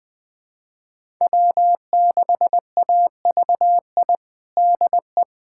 Morse code, heard as a keyed tone, about 21 wpm, 700 Hz. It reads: W6AVI DE
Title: USA Callsign DE Callsign - 5 Characters - 20wpm